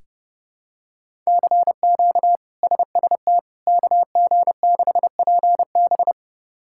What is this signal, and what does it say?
Morse code, heard as a keyed tone, about 30 wpm, 700 Hz. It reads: CQ SST KG6PB
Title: K1USN SST QSO Practice - 30wpm